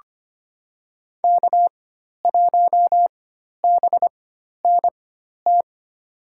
Morse code, heard as a keyed tone, about 25 wpm, spaced as for 12 wpm, 700 Hz. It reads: K1BNT
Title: USA Call Signs - 5 Characters - 4x Character Spacing - 25wpm